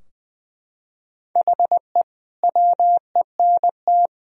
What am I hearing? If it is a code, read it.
Morse code, heard as a keyed tone, about 21 wpm, 700 Hz. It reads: HE WENT
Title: Sentences from Top 1000 Words - Review Part 2 - 20wpm